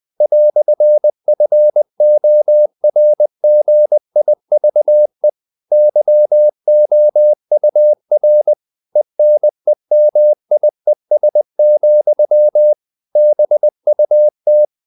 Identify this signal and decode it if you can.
Morse code, heard as a keyed tone, about 20 wpm, 600 Hz. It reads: "FORGIVE YOUR ENEMIES, BUT
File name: CW / Practice / EnglishQuotations0320